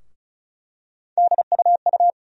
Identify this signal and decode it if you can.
Morse code, heard as a keyed tone, about 34 wpm, 700 Hz. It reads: DUU
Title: USA Call Sign Postfixes - 3 Letters 35wpm